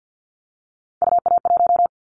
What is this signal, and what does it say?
Morse code, heard as a keyed tone, about 50 wpm, 700 Hz. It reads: UA1
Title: Callsign Prefixes for Top 100 DXCC - 50wpm